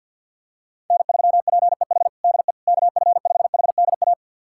Morse code, heard as a keyed tone, about 52 wpm, 700 Hz. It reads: N3PEH DE KF5HDA
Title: USA Callsign DE Callsign 50wpm